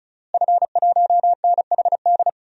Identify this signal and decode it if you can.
Morse code, heard as a keyed tone, about 36 wpm, 700 Hz. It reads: F1NHD
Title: DXCC-100 Call Signs - 5 Characters - Repeated 2x - 35wpm